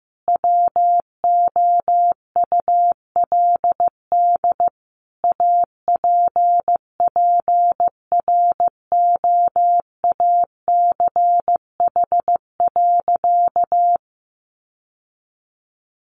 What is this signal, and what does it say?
Morse code, heard as a keyed tone, about 15 wpm, 700 Hz. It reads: WOULD APPROACH.
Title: Sentences from Top 1400 Words 15wpm